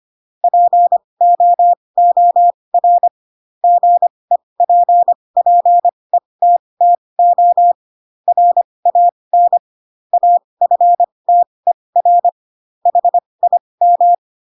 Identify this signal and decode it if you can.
Morse code, heard as a keyed tone, about 25 wpm, spaced as for 20 wpm, 700 Hz. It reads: POOR GEPPETTO RAN AFTER HIM